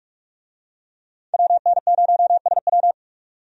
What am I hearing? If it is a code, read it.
Morse code, heard as a keyed tone, about 46 wpm, 700 Hz. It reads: WN0SW
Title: USA Call Signs - 5 Characters - 45wpm